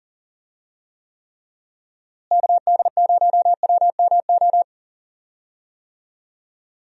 Morse code, heard as a keyed tone, about 39 wpm, 700 Hz. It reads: KD0WMO
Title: USA Call Signs - 6 Characters - Repeated 3x - 40wpm